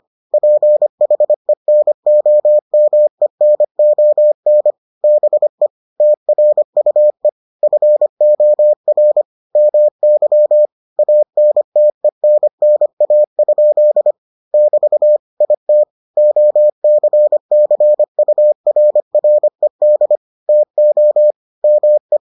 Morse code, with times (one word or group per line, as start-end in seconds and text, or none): 0.3-4.7 PHENOMENON
5.0-5.7 BE
6.0-7.3 TRUE
7.6-9.2 FOR
9.6-10.7 MY
11.0-14.1 ANTENNA?
14.5-15.2 <BT>
15.4-15.8 IT
16.2-20.2 OCCURRED
20.5-21.3 TO
21.6-22.2 ME